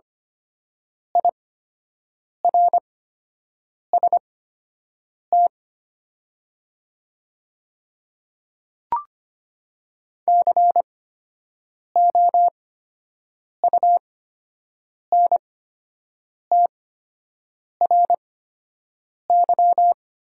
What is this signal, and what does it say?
Morse code, heard as a keyed tone, about 25 wpm, spaced as for 7 wpm, 690 Hz. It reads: IRST COUNTRY
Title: 8x Character Spacing - Top 500 Words - 25wpm